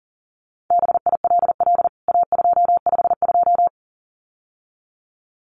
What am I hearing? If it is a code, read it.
Morse code, heard as a keyed tone, about 40 wpm, 700 Hz. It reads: BILL A252
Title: CWT Exchange 40wpm